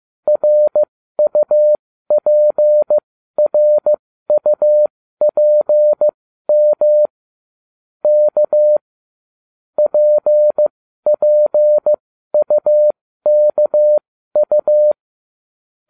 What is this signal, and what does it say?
Morse code, heard as a keyed tone, about 15 wpm, spaced as for 12 wpm, 600 Hz. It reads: RUPRUPM K PPUKU